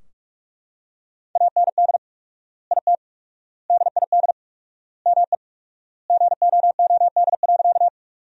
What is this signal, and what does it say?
Morse code, heard as a keyed tone, about 45 wpm, 700 Hz. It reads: AND IT DID ME GOOD.
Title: 4x Word-Spacing - Sentences from Top 200 Words - 45wpm